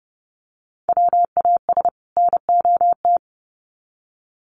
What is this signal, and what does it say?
Morse code, heard as a keyed tone, about 30 wpm, 700 Hz. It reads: WAS NOT
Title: Sets of 2 Words 30wpm